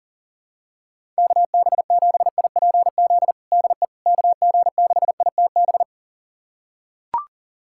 Morse code, heard as a keyed tone, about 40 wpm, 700 Hz. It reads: KB7IPZ DE KG6ITB
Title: USA Callsign DE Callsign 40wpm